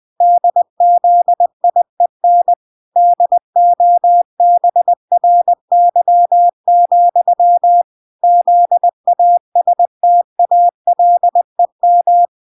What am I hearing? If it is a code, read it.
Morse code, heard as a keyed tone, about 20 wpm, 700 Hz. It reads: DZIEN DOBRY, ZASTALEM
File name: recordings / 20wpm0000